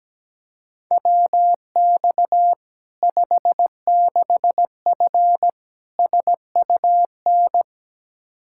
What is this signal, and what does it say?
Morse code, heard as a keyed tone, about 17 wpm, 700 Hz. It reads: WX 56F SUN
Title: Practice QSO - Exchange 2 - Info - 17wpm